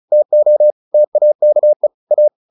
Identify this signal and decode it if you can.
Morse code, heard as a keyed tone, about 35 wpm, 600 Hz. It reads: TO TAKE A